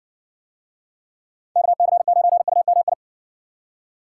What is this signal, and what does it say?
Morse code, heard as a keyed tone, about 54 wpm, 700 Hz. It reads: KC9UMI